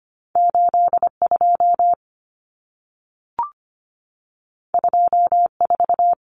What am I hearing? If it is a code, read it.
Morse code, heard as a keyed tone, about 25 wpm, 700 Hz. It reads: 82 24
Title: Common Cut Numbers - 2-Digit 25wpm